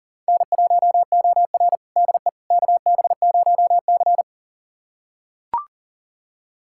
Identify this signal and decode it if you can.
Morse code, heard as a keyed tone, about 38 wpm, 700 Hz. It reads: N1OR DE KB0C
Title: USA Callsign DE Callsign - 4 Characters - 40wpm